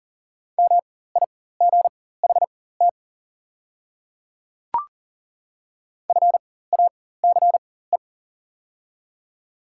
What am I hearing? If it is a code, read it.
Morse code, heard as a keyed tone, about 40 wpm, spaced as for 19 wpm, 690 Hz. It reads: MIGHT FACE
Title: 4x Character Spacing - Top 300 Words - 40wpm